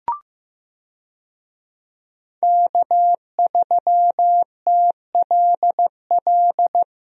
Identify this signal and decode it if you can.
Morse code, heard as a keyed tone, about 15 wpm, 700 Hz. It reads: K3TLL
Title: USA Call Signs - 5 Characters - Repeated 3x - 15wpm